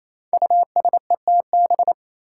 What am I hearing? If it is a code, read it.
Morse code, heard as a keyed tone, about 27 wpm, 700 Hz. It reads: USETB
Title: Five Challenging Letter-Number Combinations 28wpm